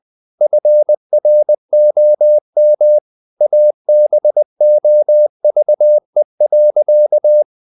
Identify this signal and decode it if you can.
Morse code, heard as a keyed tone, about 20 wpm, 600 Hz. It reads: FROM ABOVE.
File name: the old man and the sea part 2 20wpm